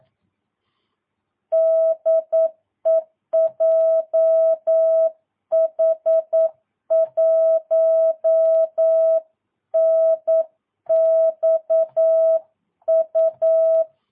Morse code, heard as a keyed tone, about 9 wpm, 650 Hz. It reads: DE JH1NXU
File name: jh8cbh / 2022cw